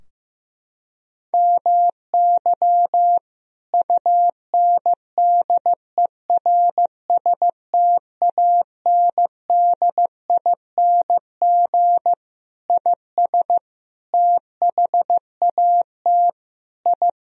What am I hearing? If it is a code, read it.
Morse code, heard as a keyed tone, about 16 wpm, 700 Hz. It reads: MY UNDERSTANDING IS THAT I
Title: Sentences from Top 1000 Words - Review Part 3 - 15wpm